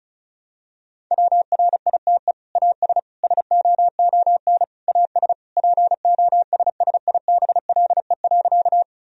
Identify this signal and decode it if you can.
Morse code, heard as a keyed tone, about 36 wpm, 700 Hz. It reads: WRITE AS SOON AS POSSIBLE.
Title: Sentences from Top 1000 Words - Review Part 1 - 35wpm